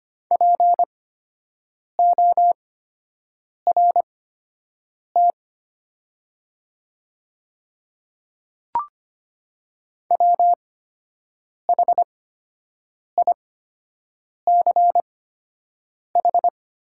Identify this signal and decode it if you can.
Morse code, heard as a keyed tone, about 25 wpm, spaced as for 7 wpm, 700 Hz. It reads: PORT WHICH
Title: 8x Character Spacing - Top 300 Words - 25wpm